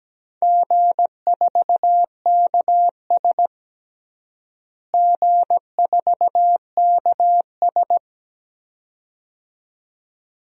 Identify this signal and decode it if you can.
Morse code, heard as a keyed tone, about 17 wpm, 700 Hz. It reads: G4KS G4KS
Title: DXCC-100 Call Signs - 4 Characters - Repeated 2x - 17wpm